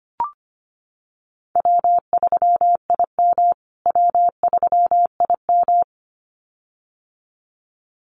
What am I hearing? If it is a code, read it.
Morse code, heard as a keyed tone, about 25 wpm, 700 Hz. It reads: W3IM W3IM
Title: USA Call Signs Repeated 2x 25wpm